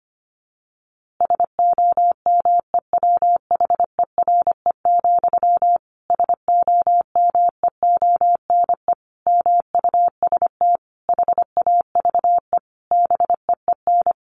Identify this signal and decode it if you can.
Morse code, heard as a keyed tone, about 25 wpm, 700 Hz. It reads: SOMEWHERE, SOMEONE MUST HAVE BEEN
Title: A Sound Of Thunder 25wpm